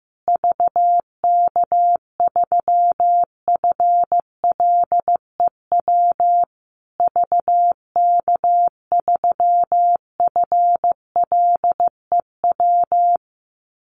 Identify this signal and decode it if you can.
Morse code, heard as a keyed tone, about 14 wpm, 700 Hz. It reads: VK3FLEW VK3FLEW
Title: USA Call Signs Repeated 3x 15wpm